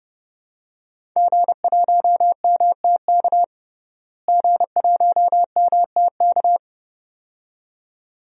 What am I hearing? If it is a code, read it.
Morse code, heard as a keyed tone, about 30 wpm, 700 Hz. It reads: G1MTK G1MTK
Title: DXCC-100 Call Signs - 5 Characters - Repeated 2x - 30wpm